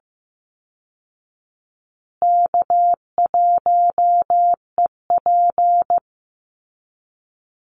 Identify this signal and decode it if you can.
Morse code, heard as a keyed tone, about 15 wpm, 700 Hz. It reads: K1EP
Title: Easy Call Signs Repeated 3x - 15wpm